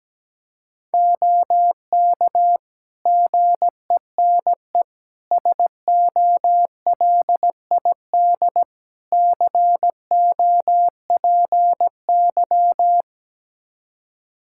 Morse code, heard as a keyed tone, about 17 wpm, 700 Hz. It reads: OK GENE SOLID COPY
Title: Practice QSO - Exchange 3 - More Info - 17wpm